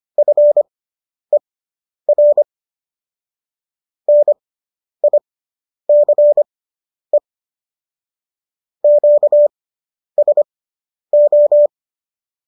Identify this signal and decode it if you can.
Morse code, heard as a keyed tone, about 25 wpm, spaced as for 10 wpm, 590 Hz. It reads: FER NICE QSO